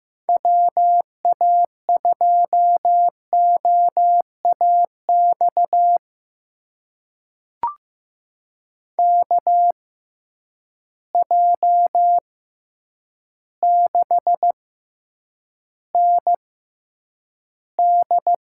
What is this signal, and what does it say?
Morse code, heard as a keyed tone, about 15 wpm, spaced as for 8 wpm, 700 Hz. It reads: WA2OAX KJ6ND
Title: USA Call Signs - 6 Characters - 6x Character Spacing - 15wpm